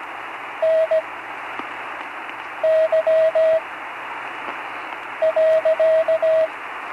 Morse code, heard as a keyed tone, about 17 wpm, spaced as for 5 wpm, 640 Hz, with radio noise. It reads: NY.
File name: CW Learning Programs / Learning Morse Code (from Tune in the World) / 35 answers